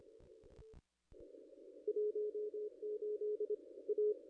Morse code, heard as a keyed tone, about 26 wpm, 420 Hz. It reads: 18 A
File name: contests / ss-cw-k5tr-op-n5rz -0200